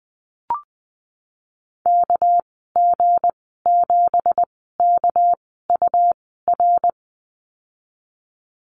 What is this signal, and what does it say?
Morse code, heard as a keyed tone, about 20 wpm, spaced as for 15 wpm, 700 Hz. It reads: KG7KUR
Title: 200 Lesson 32g - Letter k - Call Signs 20wpm